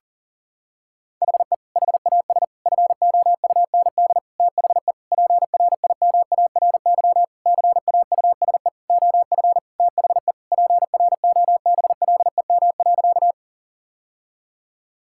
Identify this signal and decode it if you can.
Morse code, heard as a keyed tone, about 40 wpm, 700 Hz. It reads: HE HAS FOUND THE PRIMARY CAUSE OF THE PROBLEM.
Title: Sentences from Top 1000 Words - Review Part 5 - 40wpm